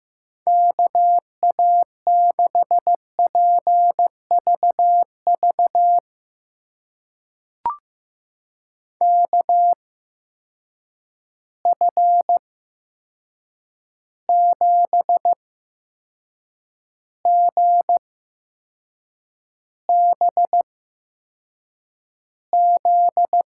Morse code, heard as a keyed tone, about 15 wpm, spaced as for 6 wpm, 700 Hz. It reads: KA6PVV KF7GBZ
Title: USA Call Signs - 6 Characters - 8x Character Spacing - 15wpm